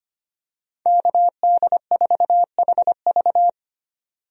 Morse code, heard as a keyed tone, about 25 wpm, 700 Hz. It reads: KD4HV
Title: USA Call Signs - 5 Characters - 25wpm